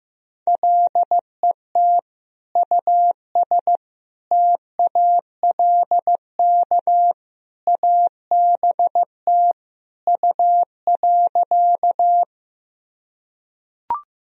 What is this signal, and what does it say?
Morse code, heard as a keyed tone, about 15 wpm, 700 Hz. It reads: LET US TALK ABT U.
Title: Sentences from Top 400 Words and Abbreviations 15wpm